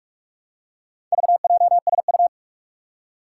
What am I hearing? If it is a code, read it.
Morse code, heard as a keyed tone, about 44 wpm, 700 Hz. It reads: VJSU
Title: Four Challenging Letter-Number Combinations 45wpm